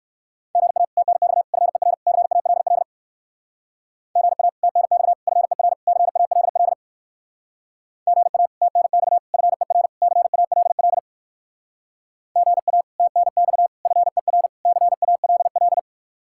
Morse code, heard as a keyed tone, about 54 wpm, 700 Hz. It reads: GA TNX FER CALL GA TNX FER CALL GA TNX FER CALL GA TNX FER CALL
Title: Speed-Racing Common QSO Phrases - 45wpm